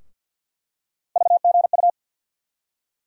Morse code, heard as a keyed tone, about 52 wpm, 700 Hz. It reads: VGU
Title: USA Call Sign Postfixes - 3 Letters 50wpm